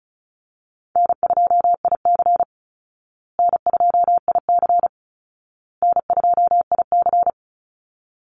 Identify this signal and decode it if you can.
Morse code, heard as a keyed tone, about 34 wpm, 700 Hz. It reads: N2IC N2IC N2IC
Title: Call Signs Repeated 2x - 35wpm